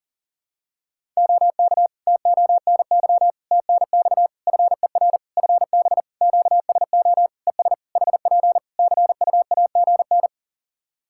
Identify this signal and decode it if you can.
Morse code, heard as a keyed tone, about 40 wpm, 700 Hz. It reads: OK TONY TNX FER FB QSO ES HP CUAGN
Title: Practice QSO - Exchange 4 - Ending - 40wpm